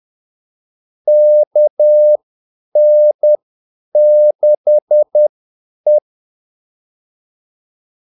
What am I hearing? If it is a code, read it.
Morse code, heard as a keyed tone, about 10 wpm, spaced as for 8 wpm, 600 Hz. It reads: KN6E
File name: Callsigns-10-8wpm